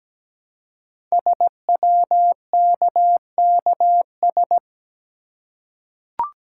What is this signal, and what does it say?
Morse code, heard as a keyed tone, about 17 wpm, 700 Hz. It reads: SWKKS
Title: Code Groups - 5 Letters - 17wpm